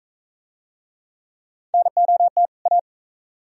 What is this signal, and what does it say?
Morse code, heard as a keyed tone, about 44 wpm, 700 Hz. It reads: NOT A